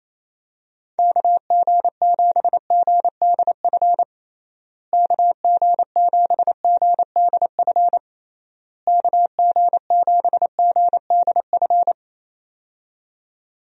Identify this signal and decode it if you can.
Morse code, heard as a keyed tone, about 27 wpm, 700 Hz. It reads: KG7GDF KG7GDF KG7GDF
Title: USA Call Signs - 6 Characters - Repeated 3x - 28wpm